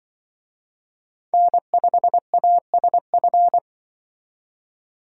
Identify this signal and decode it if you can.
Morse code, heard as a keyed tone, about 25 wpm, 700 Hz. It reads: N5ASF
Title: Speed Racing - USA 1x3 Call Signs 20wpm